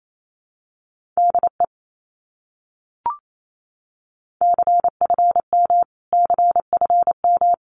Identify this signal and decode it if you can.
Morse code, heard as a keyed tone, about 28 wpm, 700 Hz. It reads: DE CFM CFM
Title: Abbreviations Repeated 3x - 28wpm